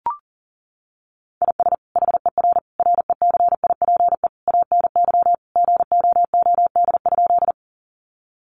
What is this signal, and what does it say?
Morse code, heard as a keyed tone, about 37 wpm, 700 Hz. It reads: IS HER RECIPE ANY GOOD?
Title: Sentences from Top 1200 Words 40wpm